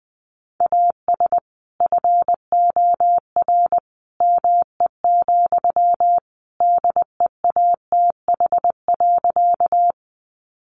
Morse code, heard as a keyed tone, about 20 wpm, 700 Hz. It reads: AS FOR ME, DEATH.
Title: Sentences from Top 1100 Words 20wpm